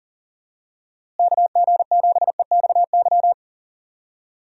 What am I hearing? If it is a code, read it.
Morse code, heard as a keyed tone, about 40 wpm, 700 Hz. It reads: KG7EXY
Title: USA Call Signs - 6 Characters - Repeated 2x - 40wpm